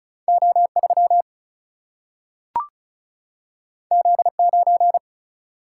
Morse code, heard as a keyed tone, about 36 wpm, 700 Hz. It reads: O3 Z9
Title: Single-Letter Single-Number 35wpm